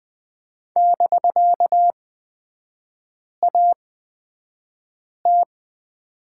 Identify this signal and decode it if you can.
Morse code, heard as a keyed tone, about 20 wpm, spaced as for 5 wpm, 700 Hz. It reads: <BK>AT
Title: 250 Lesson 40e - BK - Three Characters 20wpm